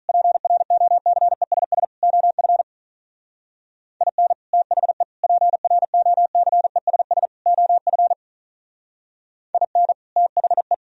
Morse code, heard as a keyed tone, about 42 wpm, 700 Hz. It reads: PROCESS OF IN THE PROCESS OF IN THE
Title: Speed-Racing - Sets of 4 Words 35wpm